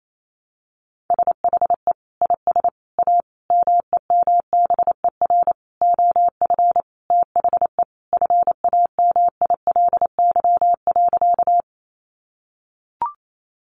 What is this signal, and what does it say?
Morse code, heard as a keyed tone, about 28 wpm, 700 Hz. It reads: SHE IS A MEMBER OF THE FAMILY.
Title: Sentences from Top 1100 Words 28wpm